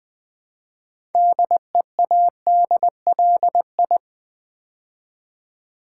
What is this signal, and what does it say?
Morse code, heard as a keyed tone, about 20 wpm, 700 Hz. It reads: DEADLI
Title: Build A Word - 3 Letters At A Time - 8-Letter Words - 20wpm